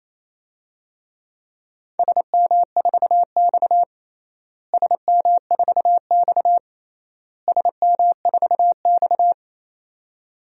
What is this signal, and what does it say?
Morse code, heard as a keyed tone, about 28 wpm, 700 Hz. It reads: SM4X SM4X SM4X
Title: DXCC-100 Call Signs - 4 Characters - Repeated 3x - 28wpm